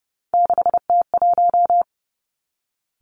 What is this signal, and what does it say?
Morse code, heard as a keyed tone, about 30 wpm, 700 Hz. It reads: 6T1
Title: Three Challenging Letter-Number Combinations 30wpm